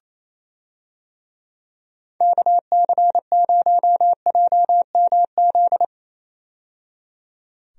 Morse code, heard as a keyed tone, about 28 wpm, 700 Hz. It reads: KC0JMZ
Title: USA Call Signs - 6 Characters - 28wpm